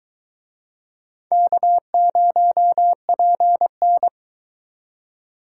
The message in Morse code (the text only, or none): K0PN